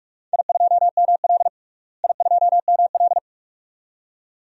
Morse code, heard as a keyed tone, about 46 wpm, 700 Hz. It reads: I2ML I2ML
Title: DXCC-100 Call Signs - 4 Characters - Repeated 2x - 45wpm